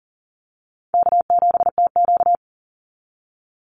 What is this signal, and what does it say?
Morse code, heard as a keyed tone, about 40 wpm, 700 Hz. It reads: K7TQ
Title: NA SOTA Chaser Callsigns 40wpm